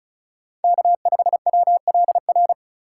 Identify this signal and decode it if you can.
Morse code, heard as a keyed tone, about 36 wpm, 700 Hz. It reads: K5WLR
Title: USA Call Signs - 5 Characters - 4x Character Spacing - 35wpm